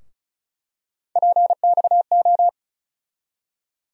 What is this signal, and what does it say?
Morse code, heard as a keyed tone, about 36 wpm, 700 Hz. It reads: PXO
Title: USA Call Sign Postfixes - 3 Letters 35wpm